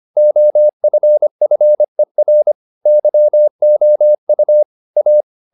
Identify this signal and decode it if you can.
Morse code, heard as a keyed tone, about 25 wpm, 600 Hz. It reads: OFFER YOU A
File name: the old man and the sea part 1 25wpm